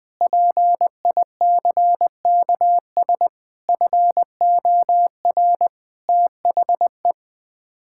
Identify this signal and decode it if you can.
Morse code, heard as a keyed tone, about 21 wpm, 700 Hz. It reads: PICKS FOR THE
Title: Sets of 9 Words - 20wpm